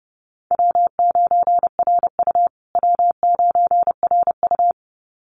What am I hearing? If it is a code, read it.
Morse code, heard as a keyed tone, about 30 wpm, 700 Hz. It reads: W9RU W9RU
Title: USA Call Signs Repeated 2x 30wpm